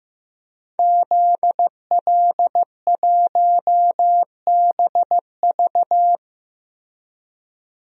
Morse code, heard as a keyed tone, about 15 wpm, 700 Hz. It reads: ZL1BV
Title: DXCC-100 Call Signs - 5 Characters - 15wpm